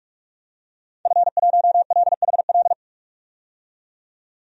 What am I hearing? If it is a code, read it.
Morse code, heard as a keyed tone, about 46 wpm, 700 Hz. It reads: F1LHL
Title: DXCC-100 Call Signs - 5 Characters - 45wpm